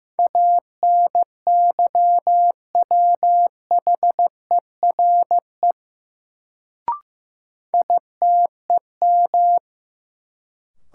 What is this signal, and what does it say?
Morse code, heard as a keyed tone, about 15 wpm, 700 Hz. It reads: ANYWHERE ITEM
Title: Sentences from Top 1400 Words - New Word Drill Alt - 15wpm